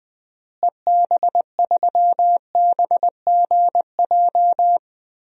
Morse code, heard as a keyed tone, about 20 wpm, 700 Hz. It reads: EB3BGJ
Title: DXCC-100 Call Signs - 6 Characters - Repeated 3x - 20wpm